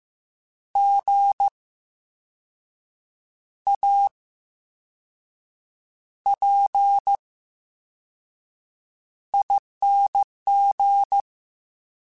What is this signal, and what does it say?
Morse code, heard as a keyed tone, about 15 wpm, spaced as for 5 wpm, 800 Hz. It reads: GAPING
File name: Special INGs-mp3s / Special INGs-mp3s / VerySlowToFaster Ing @13